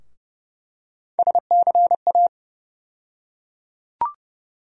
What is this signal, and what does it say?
Morse code, heard as a keyed tone, about 30 wpm, 700 Hz. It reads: SCA
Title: USA Call Sign Postfixes - 3 Letters 30wpm